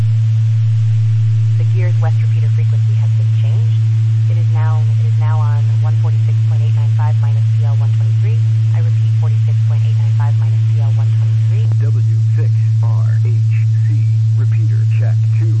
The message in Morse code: none

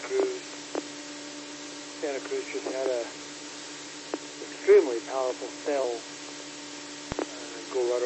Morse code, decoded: none

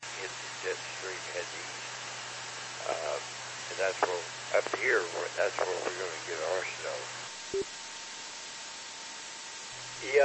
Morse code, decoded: none

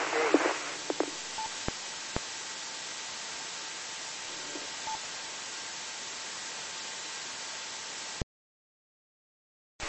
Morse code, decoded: none

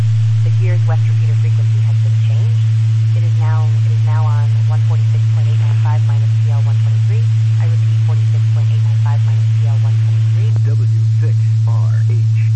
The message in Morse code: none

